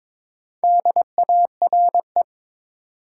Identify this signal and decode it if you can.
Morse code, heard as a keyed tone, about 22 wpm, 700 Hz. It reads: DARE
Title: Four Letter Words - 22wpm